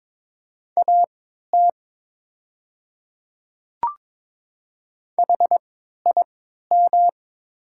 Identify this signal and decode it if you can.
Morse code, heard as a keyed tone, about 22 wpm, spaced as for 12 wpm, 690 Hz. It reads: AT HIM
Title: 3x Character Spacing - Top 100 Words - 22wpm